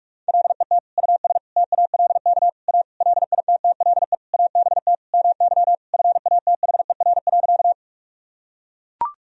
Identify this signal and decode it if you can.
Morse code, heard as a keyed tone, about 41 wpm, 700 Hz. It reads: LET US TALK A LITTLE ABT MY FATHER.
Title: Sentences from Top 400 Words and Abbreviations 45wpm